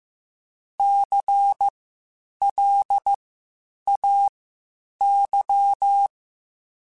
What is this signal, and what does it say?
Morse code, heard as a keyed tone, about 15 wpm, spaced as for 8 wpm, 800 Hz. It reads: CLAY
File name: Spaced Alphas / Spaced Alphas mp3s / Cs sp @13